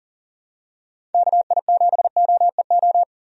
Morse code, heard as a keyed tone, about 40 wpm, 700 Hz. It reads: KI7OEO
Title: USA Call Signs - 6 Characters - 40wpm